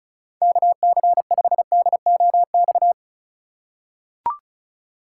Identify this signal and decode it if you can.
Morse code, heard as a keyed tone, about 34 wpm, 700 Hz. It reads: KC5DOX
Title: USA Call Signs - 2x Character Spacing - 35wpm